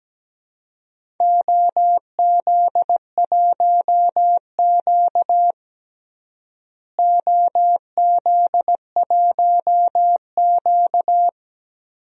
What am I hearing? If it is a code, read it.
Morse code, heard as a keyed tone, about 17 wpm, 700 Hz. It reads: OZ1Q OZ1Q
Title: DXCC-100 Call Signs - 4 Characters - Repeated 3x - 17wpm